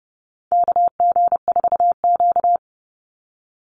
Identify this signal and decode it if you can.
Morse code, heard as a keyed tone, about 30 wpm, 700 Hz. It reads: KG4Q
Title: Call Signs - 30wpm